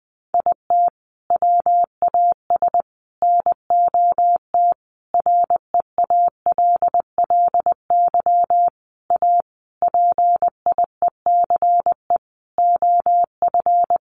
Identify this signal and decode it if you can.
Morse code, heard as a keyed tone, about 20 wpm, 700 Hz. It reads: IT WAS NOT REALLY A PIECE OF
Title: Sentences from Top 600 Words 20wpm